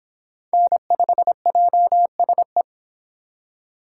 Morse code, heard as a keyed tone, about 22 wpm, 700 Hz. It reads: N5JSE